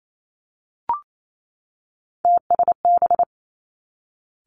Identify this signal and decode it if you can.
Morse code, heard as a keyed tone, about 28 wpm, 700 Hz. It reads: TSB
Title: Three Challenging Letter-Number Combinations 28wpm